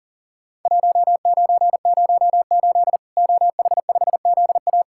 Morse code, heard as a keyed tone, about 40 wpm, 700 Hz. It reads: 1908 OH5ZA
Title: Worked All Europe DX Contest - QTC Exchange - 40wpm